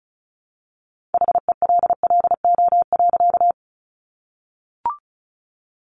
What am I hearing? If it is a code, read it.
Morse code, heard as a keyed tone, about 34 wpm, 700 Hz. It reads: HELLO.
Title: Sentences from Top 1100 Words 35wpm